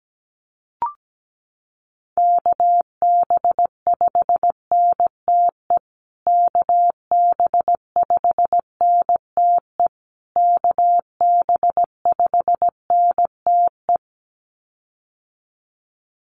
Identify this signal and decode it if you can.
Morse code, heard as a keyed tone, about 17 wpm, 700 Hz. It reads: KB5NTE KB5NTE KB5NTE
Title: USA Call Signs Repeated 3x 17wpm